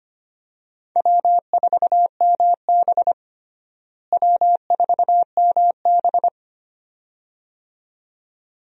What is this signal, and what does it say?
Morse code, heard as a keyed tone, about 25 wpm, 700 Hz. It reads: W4MB W4MB
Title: USA Call Signs - 4 Characters - Repeated 2x - 25wpm